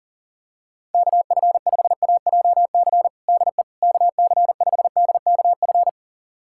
Morse code, heard as a keyed tone, about 40 wpm, 700 Hz. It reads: KF5AJC DE KC5DKF
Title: USA Callsign DE Callsign - 6 Characters - 40wpm